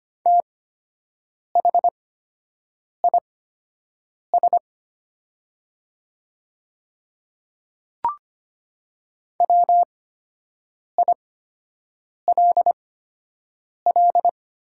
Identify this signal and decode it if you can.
Morse code, heard as a keyed tone, about 25 wpm, spaced as for 7 wpm, 700 Hz. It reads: THIS WILL
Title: 8x Character Spacing - Top 200 Words - 25wpm